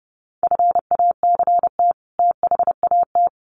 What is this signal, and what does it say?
Morse code, heard as a keyed tone, about 30 wpm, 700 Hz. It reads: FACT THAT
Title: Sets of 2 Words 30wpm